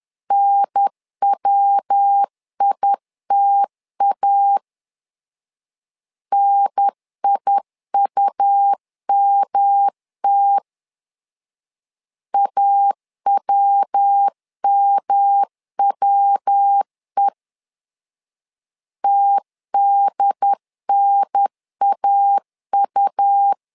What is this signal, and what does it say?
Morse code, heard as a keyed tone, about 10 wpm, 800 Hz. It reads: NWITA NIUMT AWMWE TDNAU
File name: LEZIONI / 3 MWRO / 3 MW 10wpm 1959